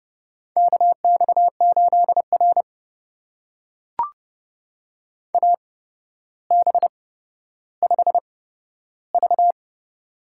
Morse code, heard as a keyed tone, about 30 wpm, spaced as for 13 wpm, 700 Hz. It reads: KX8R AB5V
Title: USA Call Signs - 4 Characters - 8x Character Spacing - 30wpm